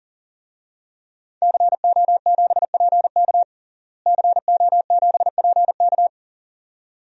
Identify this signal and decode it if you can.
Morse code, heard as a keyed tone, about 37 wpm, 700 Hz. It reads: CO7PK CO7PK
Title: DXCC-100 Call Signs - 5 Characters - Repeated 2x - 40wpm